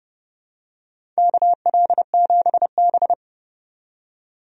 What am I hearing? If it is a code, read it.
Morse code, heard as a keyed tone, about 31 wpm, 700 Hz. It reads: KL7B
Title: DXCC-100 Call Signs - 4 Characters - 30wpm